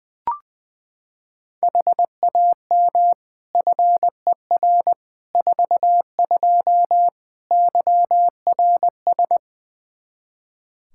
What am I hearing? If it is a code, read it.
Morse code, heard as a keyed tone, about 20 wpm, 700 Hz. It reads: HAM FER 42 YRS
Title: Practice QSO - Exchange 3 - More Info - 20wpm